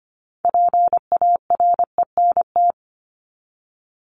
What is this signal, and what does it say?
Morse code, heard as a keyed tone, about 25 wpm, 700 Hz. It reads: PARENT
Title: Top 1751-2000 Words 25wpm